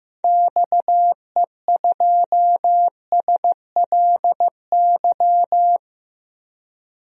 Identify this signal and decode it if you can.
Morse code, heard as a keyed tone, about 15 wpm, 700 Hz. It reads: XE2SLY
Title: DXCC-100 Call Signs - 6 Characters - Repeated 3x - 15wpm